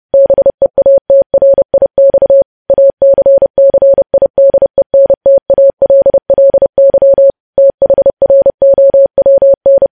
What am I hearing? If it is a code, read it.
Morse code, heard as a keyed tone, about 30 wpm, 550 Hz. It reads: BEATRIX ACCIDENTALLY THROWN
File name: podcast / News headlines-1x-30x30